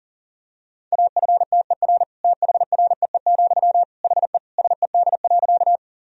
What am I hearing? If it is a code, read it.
Morse code, heard as a keyed tone, about 40 wpm, 700 Hz. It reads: AFTER THREE, HE SED.
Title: Sentences from Top 200 Words and Abbreviations 40wpm